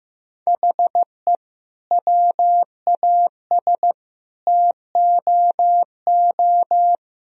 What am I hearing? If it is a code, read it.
Morse code, heard as a keyed tone, about 15 wpm, 700 Hz. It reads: HE WAS TOO
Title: Sentences from Top 1000 Words - Review Part 2 - 15wpm